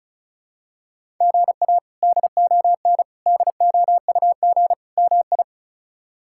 Morse code, heard as a keyed tone, about 34 wpm, 700 Hz. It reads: GA DON DOUG MI
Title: K1USN SST QSO Practice - 35wpm